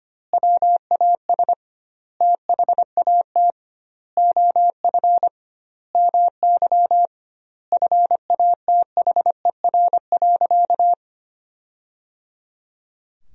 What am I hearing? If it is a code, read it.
Morse code, heard as a keyed tone, about 25 wpm, 700 Hz. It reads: WAS THAT OF MY FATHER.
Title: 2x Word-Spacing - Sentences from Top 500 Words - 25wpm